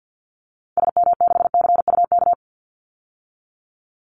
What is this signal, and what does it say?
Morse code, heard as a keyed tone, about 50 wpm, 700 Hz. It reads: SM6CUK
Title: Call Signs Repeated 2x - 50wpm